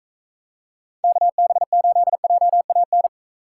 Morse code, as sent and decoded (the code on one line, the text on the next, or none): -.- -... ---.. .--- .- -.
KB8JAN